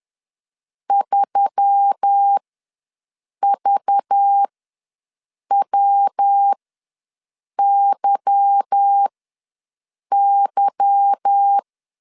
Morse code, heard as a keyed tone, about 10 wpm, spaced as for 6 wpm, 800 Hz. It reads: 3VWYY